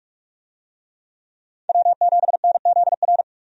Morse code, heard as a keyed tone, about 46 wpm, 700 Hz. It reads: W7NZR
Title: USA Call Signs - 5 Characters - 45wpm